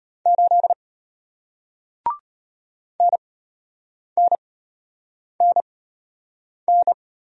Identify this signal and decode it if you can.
Morse code, heard as a keyed tone, about 34 wpm, spaced as for 8 wpm, 700 Hz. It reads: 8 NNNN